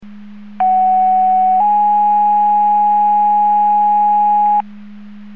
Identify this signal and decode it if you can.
Two-tone page, heard as a one-second tone, then a three-second tone, about 750 then 830 Hz. It is heard from 0.6 to 4.6 s.